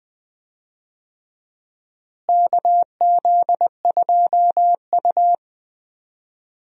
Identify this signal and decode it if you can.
Morse code, heard as a keyed tone, about 21 wpm, 700 Hz. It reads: KZ2U
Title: USA Call Signs - 4 Characters - Repeated 2x - 20wpm